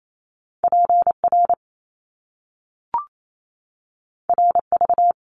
Morse code, heard as a keyed tone, about 28 wpm, 700 Hz. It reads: PR RV
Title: Top Two Letter Combinations in Words 28wpm